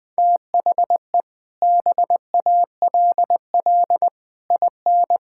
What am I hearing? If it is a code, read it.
Morse code, heard as a keyed tone, about 20 wpm, 700 Hz. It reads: THE BALL IN
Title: Sets of 8 Words - 20wpm